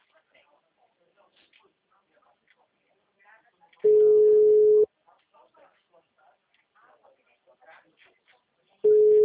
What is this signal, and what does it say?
Ringback tone (Europe, 425 Hz), with speech.